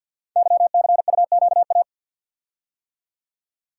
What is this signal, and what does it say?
Morse code, heard as a keyed tone, about 50 wpm, 700 Hz. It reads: YCUQA
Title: Code Groups - 5 Letters - 50wpm